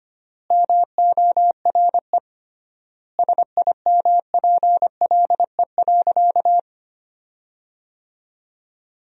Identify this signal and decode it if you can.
Morse code, heard as a keyed tone, about 25 wpm, 700 Hz. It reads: MORE SIMPLE.
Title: 3x Word-Spacing - Sentences from Top 500 Words - 25wpm